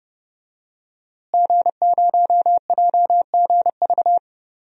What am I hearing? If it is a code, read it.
Morse code, heard as a keyed tone, about 30 wpm, 700 Hz. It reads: G0JGV
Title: DXCC-100 Call Signs - 5 Characters - Repeated 3x - 30wpm